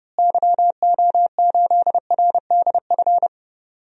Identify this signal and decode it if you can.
Morse code, heard as a keyed tone, about 30 wpm, 700 Hz. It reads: YO8RDF
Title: DXCC-100 Call Signs - 6 Characters - Repeated 2x - 30wpm